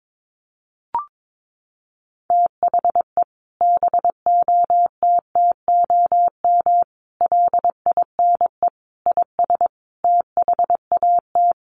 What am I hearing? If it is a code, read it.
Morse code, heard as a keyed tone, about 22 wpm, 700 Hz. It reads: THE BOTTOM LINE IS THAT
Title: Sets of 5 Words Encore 22wpm